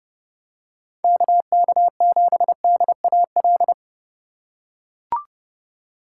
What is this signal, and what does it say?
Morse code, heard as a keyed tone, about 30 wpm, 700 Hz. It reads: KK7DAL